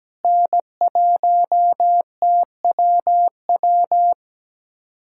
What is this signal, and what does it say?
Morse code, heard as a keyed tone, about 17 wpm, 700 Hz. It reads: N1TWW
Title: USA Callsign DE Callsign - 5 Characters - 17wpm